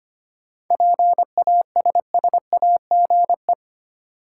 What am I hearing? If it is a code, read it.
Morse code, heard as a keyed tone, about 25 wpm, 700 Hz. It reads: PASSAGE
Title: Seven Letter Words - 25wpm